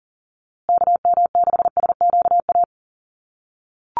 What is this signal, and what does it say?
Morse code, heard as a keyed tone, about 40 wpm, 700 Hz. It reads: KM6SQA
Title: USA Call Signs Repeated 2x 40wpm